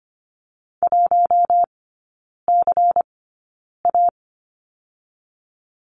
Morse code, heard as a keyed tone, about 25 wpm, spaced as for 9 wpm, 700 Hz. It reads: 1CA
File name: course / 25:10wpm / 093 Lesson 16e - Letter c - Three Characters 25wpm